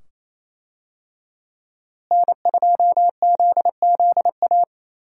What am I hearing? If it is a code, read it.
Morse code, heard as a keyed tone, about 28 wpm, 700 Hz. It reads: N2ZZA
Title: USA Call Signs - 5 Characters - Repeated 2x - 28wpm